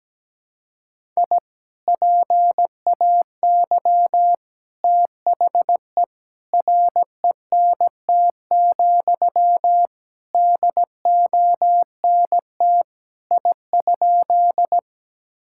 Morse code, heard as a keyed tone, about 17 wpm, 700 Hz. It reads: I PAY THE RENT, DONT I?
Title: Sentences from Top 1000 Words - Review Part 5 - 17wpm